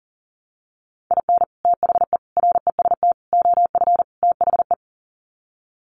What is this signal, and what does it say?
Morse code, heard as a keyed tone, about 40 wpm, 700 Hz. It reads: IN THE REST OF THE
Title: Sets of 5 Words 40wpm